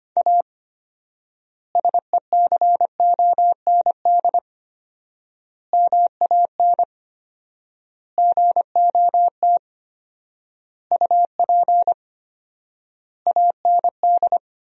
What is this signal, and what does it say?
Morse code, heard as a keyed tone, about 25 wpm, 700 Hz. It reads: A SECOND MAN GOT UP AND
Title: 4x Word-Spacing - Sentences from Top 300 Words - 25wpm